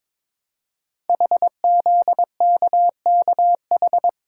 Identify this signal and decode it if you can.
Morse code, heard as a keyed tone, about 21 wpm, 700 Hz. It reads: HZKKH
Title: Code Groups - 5 Letters - 22wpm